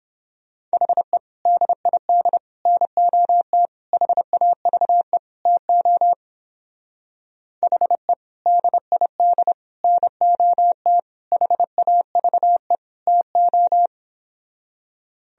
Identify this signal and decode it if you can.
Morse code, heard as a keyed tone, about 29 wpm, 700 Hz. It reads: HE DID NOT HAVE TO HE DID NOT HAVE TO